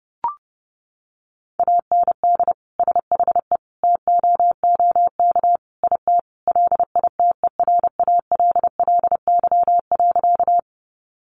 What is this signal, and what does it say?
Morse code, heard as a keyed tone, about 30 wpm, 700 Hz. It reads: AND SHE TOOK IT LITERALLY.
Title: Sentences from Top 1400 Words 30wpm